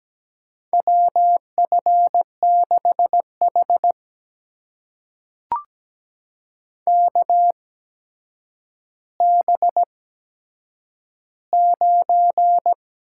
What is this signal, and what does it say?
Morse code, heard as a keyed tone, about 17 wpm, spaced as for 8 wpm, 700 Hz. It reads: WF6H KB9
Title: USA Call Signs - 4 Characters - 8x Character Spacing - 17wpm